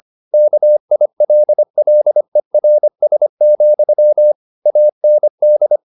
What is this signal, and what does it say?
Morse code, heard as a keyed tone, about 25 wpm, 600 Hz. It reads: KILLERS, AND